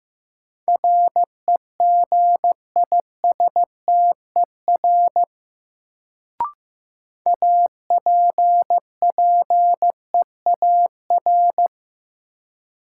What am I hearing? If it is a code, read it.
Morse code, heard as a keyed tone, about 16 wpm, 700 Hz. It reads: REGISTER APPEAR
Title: Sentences from Top 1200 Words - New Word Drill Alt - 15wpm